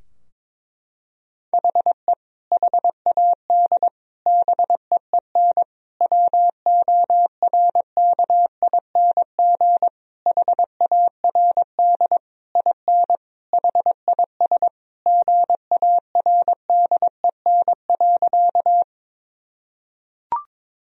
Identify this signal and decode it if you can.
Morse code, heard as a keyed tone, about 22 wpm, 700 Hz. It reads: HE HAD BEEN WORKING HARD IN HIS GARDEN.
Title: Sentences from Top 1000 Words - Review Part 4 - 22wpm